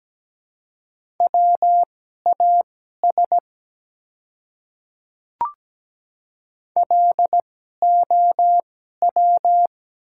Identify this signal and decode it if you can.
Morse code, heard as a keyed tone, about 17 wpm, spaced as for 12 wpm, 700 Hz. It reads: WAS LOW
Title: 2x Character Spacing - Top 200 Words - 17wpm